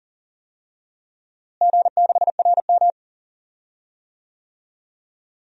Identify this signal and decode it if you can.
Morse code, heard as a keyed tone, about 40 wpm, 700 Hz. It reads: G6RM
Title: DXCC-100 Call Signs - 4 Characters - Repeated 3x - 40wpm